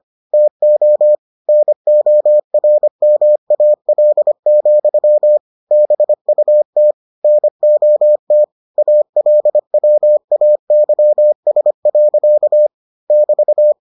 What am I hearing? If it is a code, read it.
Morse code, heard as a keyed tone, about 25 wpm, 600 Hz. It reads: TO NORMAL, BUT NOT ALWAYS. <BT>